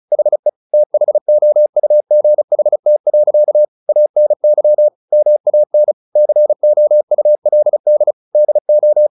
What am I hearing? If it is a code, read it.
Morse code, heard as a keyed tone, about 34 wpm, 600 Hz. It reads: HE THOUGHT. ANY MAN COULD DO
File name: the old man and the sea 2 35wpm